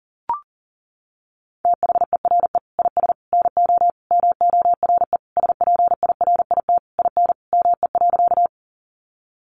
Morse code, heard as a keyed tone, about 40 wpm, 700 Hz. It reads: THERE IS NO MORE SPIRIT IN ME.
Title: Sentences from Top 1400 Words 40wpm